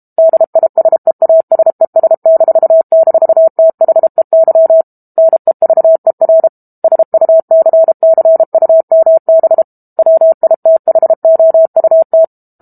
Morse code, heard as a keyed tone, about 32 wpm, 650 Hz. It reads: DISEASES--THEY NEVER SUCCUMB WITHOUT
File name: Book Two / 06-Chapter 6 - 32 WPM